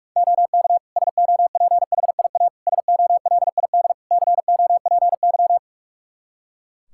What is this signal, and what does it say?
Morse code, heard as a keyed tone, about 46 wpm, 700 Hz. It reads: OK SOPHIA SOLID COPY
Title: Practice QSO - Exchange 3 - More Info - 45wpm